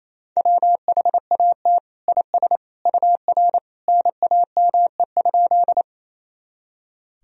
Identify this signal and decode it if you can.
Morse code, heard as a keyed tone, about 27 wpm, 700 Hz. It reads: WHAT IS UR NAME?
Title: Sentences from Top 200 Words and Abbreviations 28wpm